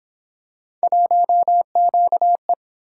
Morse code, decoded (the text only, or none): 1QE